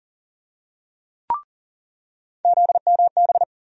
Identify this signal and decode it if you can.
Morse code, heard as a keyed tone, about 37 wpm, 700 Hz. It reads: ZMB
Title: USA Call Sign Postfixes - 3 Letters 40wpm